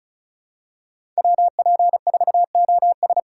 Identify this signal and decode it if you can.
Morse code, heard as a keyed tone, about 36 wpm, 700 Hz. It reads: WP4OS
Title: USA Call Signs - 5 Characters - 4x Character Spacing - 35wpm